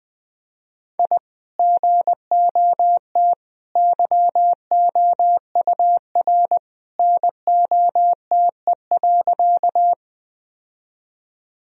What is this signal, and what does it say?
Morse code, heard as a keyed tone, about 21 wpm, 700 Hz. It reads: I GOT YOUR NOTE.
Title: Sentences from Top 1000 Words - Review Part 2 - 20wpm